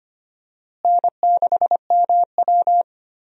Morse code, heard as a keyed tone, about 25 wpm, 700 Hz. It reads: N6MW
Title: USA Call Signs - 4 Characters - Repeated 3x - 25wpm